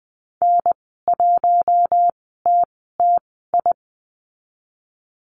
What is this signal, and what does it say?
Morse code, heard as a keyed tone, about 20 wpm, spaced as for 14 wpm, 700 Hz. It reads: N1TTI
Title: 042 Lesson 8g - Number 1 - Call Signs 20wpm